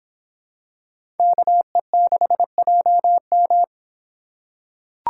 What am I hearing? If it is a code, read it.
Morse code, heard as a keyed tone, about 25 wpm, 700 Hz. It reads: KE6JM